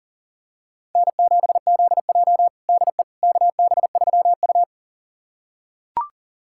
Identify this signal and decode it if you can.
Morse code, heard as a keyed tone, about 40 wpm, 700 Hz. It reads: N7ZJ DE KB3U
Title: USA Callsign DE Callsign - 4 Characters - 40wpm